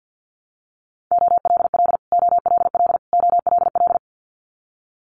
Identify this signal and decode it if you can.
Morse code, heard as a keyed tone, about 50 wpm, 700 Hz. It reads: OLL OLL OLL
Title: Multi-letter Phonemes Repeated 3x 50wpm